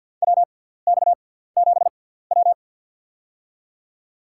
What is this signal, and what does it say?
Morse code, heard as a keyed tone, about 52 wpm, spaced as for 17 wpm, 700 Hz. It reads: WX7W
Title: USA Call Signs - 4 Characters - 6x Character Spacing - 50wpm